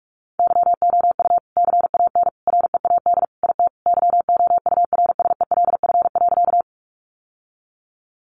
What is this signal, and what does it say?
Morse code, heard as a keyed tone, about 44 wpm, 700 Hz. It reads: YOU CAN READ IT YOURSELF.
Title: Sentences from Top 600 Words 45wpm